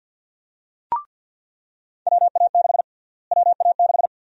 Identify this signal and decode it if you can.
Morse code, heard as a keyed tone, about 48 wpm, 700 Hz. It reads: WA6 WA6
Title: USA Call Sign Prefixes - Repeated 2x - 50wpm